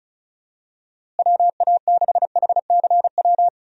Morse code, heard as a keyed tone, about 36 wpm, 700 Hz. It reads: WA6HCW
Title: USA Call Signs - 6 Characters - Repeated 2x - 35wpm